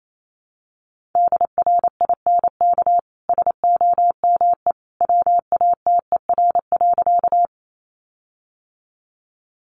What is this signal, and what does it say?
Morse code, heard as a keyed tone, about 28 wpm, 700 Hz. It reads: DRINK SOME WATER.
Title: Sentences from Top 1500 Words 28wpm